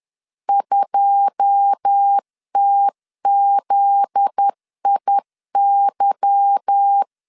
Morse code, heard as a keyed tone, about 10 wpm, 800 Hz. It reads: 2TZIY
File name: CW / DET / 1 9 1500C S A 10wpm 4511